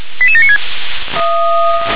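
Five-tone selective call (ZVEI), heard as quick runs of five tones, tones of about 70 ms. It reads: 88796
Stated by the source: Five-tone ZVEI selective call, received via SDR.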